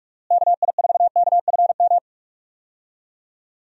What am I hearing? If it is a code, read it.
Morse code, heard as a keyed tone, about 43 wpm, 700 Hz. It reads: KI4KFM
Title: USA Call Signs - 6 Characters - Repeated 3x - 45wpm